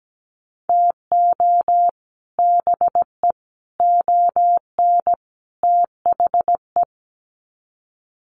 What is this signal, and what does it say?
Morse code, heard as a keyed tone, about 17 wpm, 700 Hz. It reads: TO BE ON THE
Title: Sets of 4 Words 17wpm